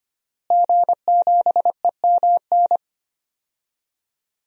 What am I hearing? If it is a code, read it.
Morse code, heard as a keyed tone, about 25 wpm, 700 Hz. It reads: G7EMN
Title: DXCC-100 Call Signs - 5 Characters - Repeated 3x - 25wpm